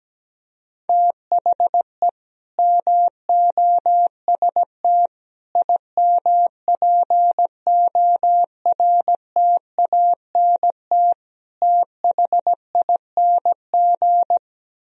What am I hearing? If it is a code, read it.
Morse code, heard as a keyed tone, about 17 wpm, 700 Hz. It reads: THE MOST IMPORTANT THING
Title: Speed-Racing - Sets of 4 Words 17wpm